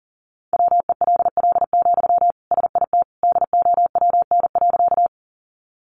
Morse code, heard as a keyed tone, about 37 wpm, 700 Hz. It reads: WELL, SIT DOWN.
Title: Sentences from Top 500 Words 40wpm